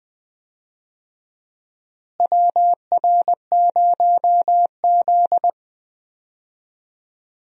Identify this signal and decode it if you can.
Morse code, heard as a keyed tone, about 19 wpm, 700 Hz. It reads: WR0Z
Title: USA Call Signs - 4 Characters - Repeated 2x - 20wpm